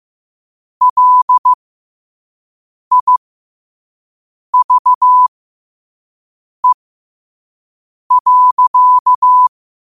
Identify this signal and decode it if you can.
Morse code, heard as a keyed tone, about 15 wpm, spaced as for 5 wpm, 1000 Hz. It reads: LIVE.